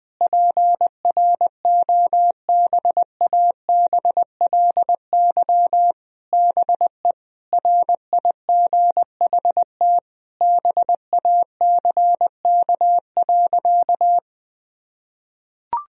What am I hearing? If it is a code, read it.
Morse code, heard as a keyed tone, about 20 wpm, 700 Hz. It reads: PROBABLY BE RIGHT BACK.
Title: Sentences from Top 1000 Words - Review Part 4 - 20wpm